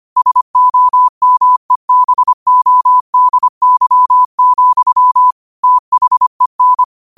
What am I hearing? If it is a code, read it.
Morse code, heard as a keyed tone, about 25 wpm, 1000 Hz. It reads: SOMEBODY, THEN
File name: qotdfiles / 25wpm